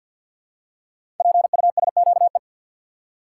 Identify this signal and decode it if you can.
Morse code, heard as a keyed tone, about 48 wpm, 700 Hz. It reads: PUSQE
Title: Code Groups - 5 Letters - 50wpm